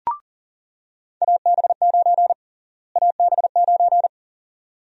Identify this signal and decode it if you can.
Morse code, heard as a keyed tone, about 37 wpm, 700 Hz. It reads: AB9 AB9
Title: USA Call Sign Prefixes - Repeated 2x - 40wpm